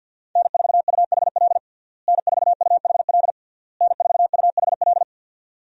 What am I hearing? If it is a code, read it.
Morse code, heard as a keyed tone, about 53 wpm, 690 Hz. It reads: N4UHL N4UHL N4UHL
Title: USA Call Signs - 5 Characters - Repeated 3x - 50wpm